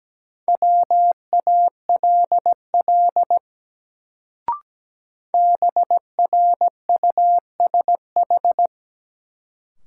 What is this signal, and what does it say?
Morse code, heard as a keyed tone, about 17 wpm, 700 Hz. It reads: WALL BRUSH
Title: Sentences from Top 1100 Words - New Word Drill Alt - 17wpm